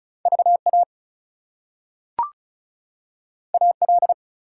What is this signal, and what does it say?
Morse code, heard as a keyed tone, about 36 wpm, 700 Hz. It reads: VA AL
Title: US States and Canadian Provinces 35wpm